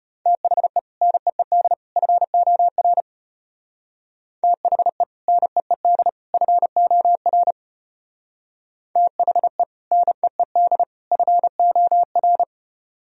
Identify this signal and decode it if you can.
Morse code, heard as a keyed tone, about 33 wpm, 700 Hz. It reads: THE NEED FOR THE NEED FOR THE NEED FOR